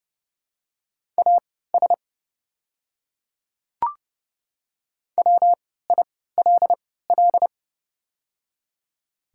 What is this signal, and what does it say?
Morse code, heard as a keyed tone, about 30 wpm, spaced as for 17 wpm, 700 Hz. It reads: AS WILL
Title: 3x Character Spacing - Top 100 Words - 30wpm